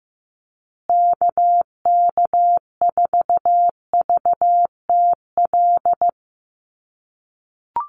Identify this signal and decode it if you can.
Morse code, heard as a keyed tone, about 15 wpm, 700 Hz. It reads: KK4VTL
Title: USA Call Signs Repeated 2x 15wpm